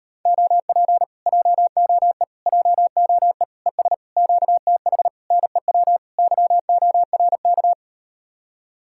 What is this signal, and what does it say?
Morse code, heard as a keyed tone, about 38 wpm, 700 Hz. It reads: OP JOE JOE ES QTH NEW YORK